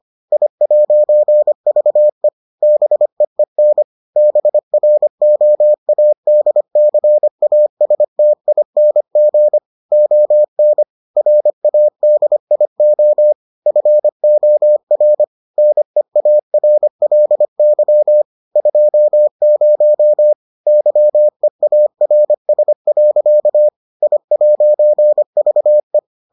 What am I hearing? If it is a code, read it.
Morse code, heard as a keyed tone, about 25 wpm, 600 Hz. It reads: I'VE BEEN BROADCASTING ON RADIO FOR NEARLY 20 YEARS. I'VE